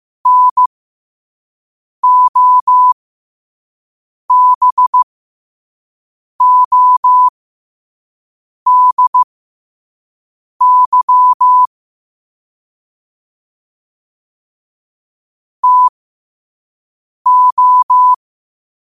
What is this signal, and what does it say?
Morse code, heard as a keyed tone, about 15 wpm, spaced as for 5 wpm, 1000 Hz. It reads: NOBODY TO
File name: qotdfiles / 5wpm